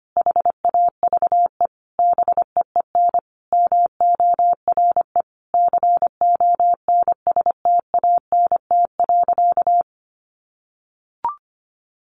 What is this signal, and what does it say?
Morse code, heard as a keyed tone, about 25 wpm, 700 Hz. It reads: HAVE BEEN MORE CONSTANT.
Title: Sentences from Top 1500 Words 25wpm